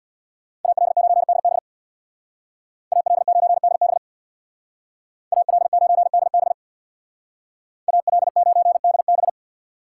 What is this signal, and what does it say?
Morse code, heard as a keyed tone, about 63 wpm, 700 Hz. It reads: AL9DB AL9DB AL9DB AL9DB